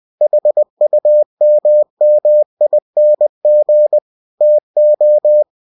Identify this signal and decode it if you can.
Morse code, heard as a keyed tone, about 20 wpm, 600 Hz. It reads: HUMMING TO
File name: cw-books / winnie-the-pooh / winnie-20wpm-page-0015